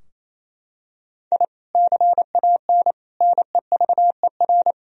Morse code, heard as a keyed tone, about 28 wpm, 700 Hz. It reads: I CAN NEVER
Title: Sentences from Top 1000 Words - Review Part 3 - 28wpm